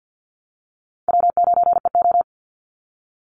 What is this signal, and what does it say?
Morse code, heard as a keyed tone, about 50 wpm, 700 Hz. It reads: W9EO
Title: Easy Call Signs - 50wpm